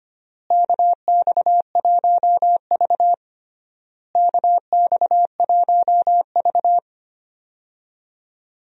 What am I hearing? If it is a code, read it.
Morse code, heard as a keyed tone, about 25 wpm, 700 Hz. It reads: KX1V KX1V
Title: USA Call Signs - 4 Characters - Repeated 3x - 25wpm